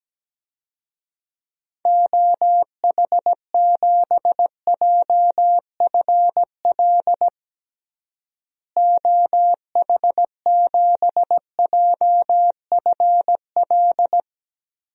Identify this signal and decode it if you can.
Morse code, heard as a keyed tone, about 17 wpm, 700 Hz. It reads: OH7JFL OH7JFL
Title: DXCC-100 Call Signs - 6 Characters - Repeated 3x - 17wpm